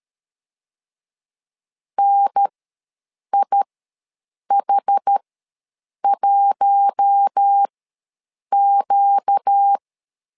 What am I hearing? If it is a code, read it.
Morse code, heard as a keyed tone, about 13 wpm, spaced as for 7 wpm, 800 Hz. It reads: NIH1Q
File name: LEZIONI / 4 Q1 ws 12wpm 2129